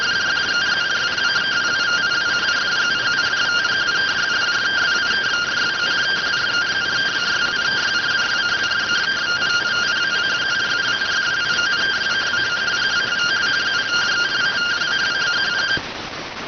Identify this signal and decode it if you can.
Two-tone teleprinter signal, about 170 Hz shift.
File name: findings / MR 2 Pel Sag T2 SSFSE 0009.gen-38 rx